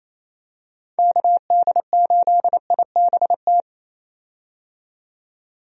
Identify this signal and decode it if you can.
Morse code, heard as a keyed tone, about 28 wpm, 700 Hz. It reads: KD8IBT
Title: USA Call Signs - 6 Characters - 28wpm